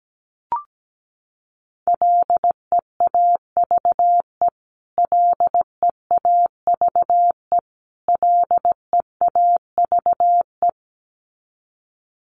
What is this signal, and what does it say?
Morse code, heard as a keyed tone, about 17 wpm, 700 Hz. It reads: LEAVE LEAVE LEAVE
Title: Top 500 Words Repeated 17wpm